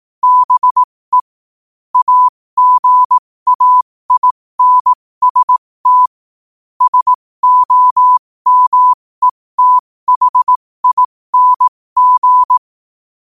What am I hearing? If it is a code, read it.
Morse code, heard as a keyed tone, about 18 wpm, spaced as for 15 wpm, 1000 Hz. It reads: BE AGAINST SOMETHING
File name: qotdfiles / 15wpm